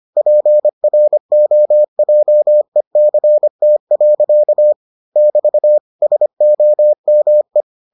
Morse code, heard as a keyed tone, about 25 wpm, 600 Hz. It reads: PROJECT. <BT> SOME